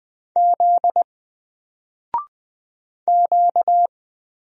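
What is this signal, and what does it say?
Morse code, heard as a keyed tone, about 20 wpm, 700 Hz. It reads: Z Q
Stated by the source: Single letter and number Morse drill at 20 wpm, from N4LCD.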